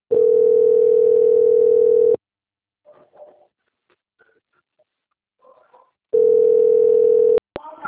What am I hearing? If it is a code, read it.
Ringback tone (North America), 440 + 480 Hz.